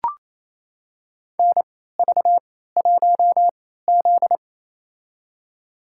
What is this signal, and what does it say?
Morse code, heard as a keyed tone, about 28 wpm, spaced as for 16 wpm, 700 Hz. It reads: NV1Z
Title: USA Call Signs - 3x Character Spacing - 28wpm